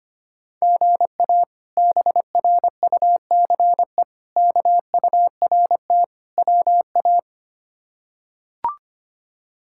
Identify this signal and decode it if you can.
Morse code, heard as a keyed tone, about 25 wpm, 700 Hz. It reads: GA BRUCE KURT WA
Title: K1USN SST QSO Practice - 25wpm